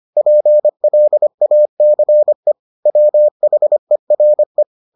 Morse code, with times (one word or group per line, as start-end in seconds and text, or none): 0.2-2.5 PLACE
2.9-4.6 WHERE